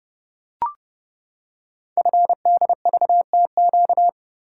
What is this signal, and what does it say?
Morse code, heard as a keyed tone, about 30 wpm, 700 Hz. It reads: FDVTQ
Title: Code Groups - 5 Letters - 30wpm